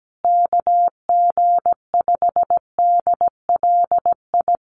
Morse code, heard as a keyed tone, about 17 wpm, 700 Hz. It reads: KG5DLI
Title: USA Call Signs Repeated 3x 17wpm